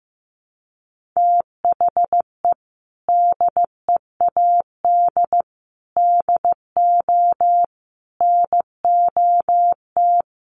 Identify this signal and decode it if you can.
Morse code, heard as a keyed tone, about 15 wpm, 700 Hz. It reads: THE DEAD DO NOT
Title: Sentences from Top 1500 Words 15wpm